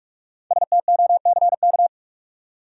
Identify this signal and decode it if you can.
Morse code, heard as a keyed tone, about 45 wpm, 700 Hz. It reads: STOCK
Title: Five Letter Words - 45wpm